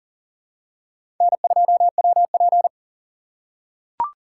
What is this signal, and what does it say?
Morse code, heard as a keyed tone, about 40 wpm, 700 Hz. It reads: N2WP
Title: USA Call Signs - 4 Characters - 3x Character Spacing - 40wpm